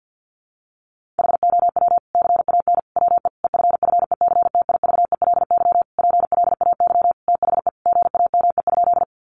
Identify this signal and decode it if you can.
Morse code, heard as a keyed tone, about 50 wpm, 700 Hz. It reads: HOW CAN WE EFFECTIVELY PLAY THE GAME?
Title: Sentences from Top 1400 Words 50wpm